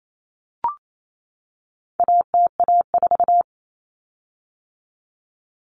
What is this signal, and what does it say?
Morse code, heard as a keyed tone, about 28 wpm, 700 Hz. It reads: ATA4
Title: Common Cut Numbers - 4-Digit 28wpm